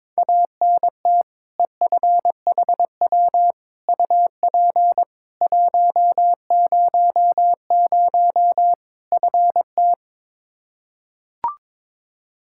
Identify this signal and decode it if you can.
Morse code, heard as a keyed tone, about 22 wpm, 700 Hz. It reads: ANT EFHW UP 100 FT
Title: Practice QSO - Exchange 2 - Info - 22wpm